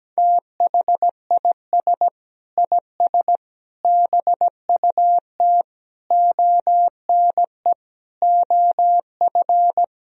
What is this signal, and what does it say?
Morse code, heard as a keyed tone, about 17 wpm, 700 Hz. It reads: THIS IS BUT ONE OF
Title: Sentences from Top 1000 Words - Review Part 2 - 17wpm